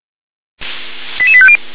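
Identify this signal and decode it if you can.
Five-tone selective call (ZVEI), heard as quick runs of five tones, tones of about 70 ms. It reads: 88850
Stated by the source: ZVEI five-tone selective call 88776 received by SDR.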